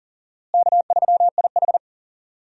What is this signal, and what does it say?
Morse code, heard as a keyed tone, about 40 wpm, 700 Hz. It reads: K3IH
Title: USA Call Signs - 4 Characters - Repeated 3x - 40wpm